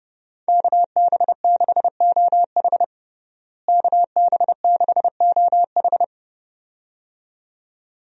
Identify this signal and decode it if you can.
Morse code, heard as a keyed tone, about 30 wpm, 700 Hz. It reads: KB6OH KB6OH
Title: USA Call Signs - 5 Characters - Repeated 2x - 30wpm